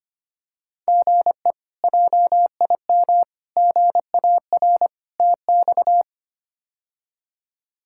Morse code, heard as a keyed tone, about 25 wpm, 700 Hz. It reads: GE JIM GAR TX
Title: K1USN SST QSO Practice - 25wpm